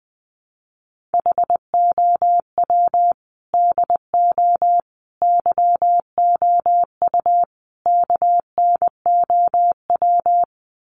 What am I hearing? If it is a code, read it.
Morse code, heard as a keyed tone, about 20 wpm, 700 Hz. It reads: HOW DO YOU KNOW
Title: Sets of 4 Words 20wpm